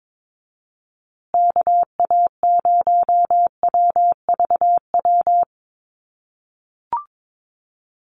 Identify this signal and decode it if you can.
Morse code, heard as a keyed tone, about 22 wpm, 700 Hz. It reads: KA0WVW
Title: USA Call Signs Repeated 3x 22wpm